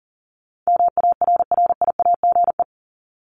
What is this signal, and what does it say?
Morse code, heard as a keyed tone, about 40 wpm, 700 Hz. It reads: MARRIAGE
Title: Top 751-1000 Words 40wpm